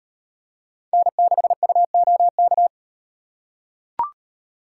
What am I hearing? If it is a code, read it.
Morse code, heard as a keyed tone, about 36 wpm, 700 Hz. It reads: N6UOK